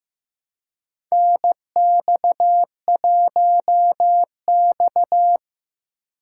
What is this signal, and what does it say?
Morse code, heard as a keyed tone, about 15 wpm, 700 Hz. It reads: NX1X
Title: USA Call Signs - 4 Characters - 3x Character Spacing - 15wpm